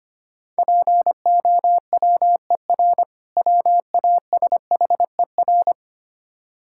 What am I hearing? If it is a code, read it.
Morse code, heard as a keyed tone, about 25 wpm, 700 Hz. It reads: POWER WASHER
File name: practice / 25wpm / Difficult Words - Speed-Racing with Hints - 25wpm